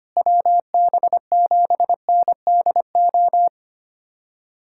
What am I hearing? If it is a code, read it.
Morse code, heard as a keyed tone, about 25 wpm, 700 Hz. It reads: WB7NDO
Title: USA Callsign DE Callsign 25wpm